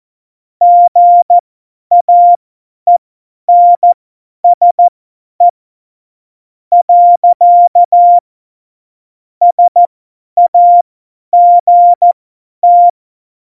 Morse code, heard as a keyed tone, about 14 wpm, spaced as for 10 wpm, 700 Hz. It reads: GAENSE . SAGT
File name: Morsekurs / Witze / Witz 21. August 2018, 50 F70